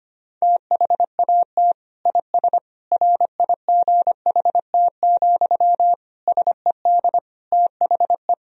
Morse code, heard as a keyed tone, about 25 wpm, 700 Hz. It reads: THAT IS RIGHT, SED THE
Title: Sentences from Top 200 Words and Abbreviations 25wpm